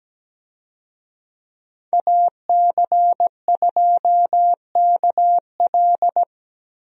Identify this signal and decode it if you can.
Morse code, heard as a keyed tone, about 17 wpm, 700 Hz. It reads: AC2KL
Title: USA Call Signs - 5 Characters - 17wpm